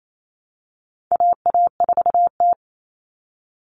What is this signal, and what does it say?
Morse code, heard as a keyed tone, about 28 wpm, 700 Hz. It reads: AA4T
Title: All Cut Numbers - 4-Digit 28wpm